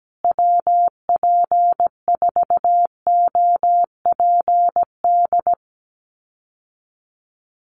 Morse code, heard as a keyed tone, about 17 wpm, 700 Hz. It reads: WP4OPD
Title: USA Call Signs Repeated 2x 17wpm